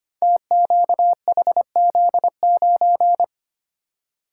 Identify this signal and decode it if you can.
Morse code, heard as a keyed tone, about 25 wpm, 700 Hz. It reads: TQHZ9
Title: Code Groups - 5 Letter:Numbers - 25wpm